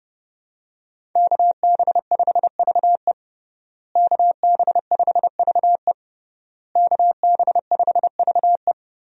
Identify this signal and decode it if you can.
Morse code, heard as a keyed tone, about 31 wpm, 700 Hz. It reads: KB5VE KB5VE KB5VE
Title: USA Call Signs - 5 Characters - Repeated 3x - 30wpm